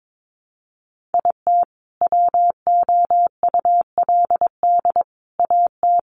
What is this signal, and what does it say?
Morse code, heard as a keyed tone, about 22 wpm, 700 Hz. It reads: IT WOULD AT
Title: Sentences from Top 700 Words 22wpm